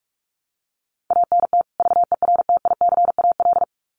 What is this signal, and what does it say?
Morse code, heard as a keyed tone, about 44 wpm, 700 Hz. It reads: ANT VERTICAL
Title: Common QSO Phrases 45wpm